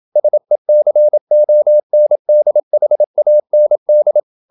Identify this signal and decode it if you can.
Morse code, heard as a keyed tone, about 27 wpm, 600 Hz. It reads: SECONDHAND
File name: Never 0027wpm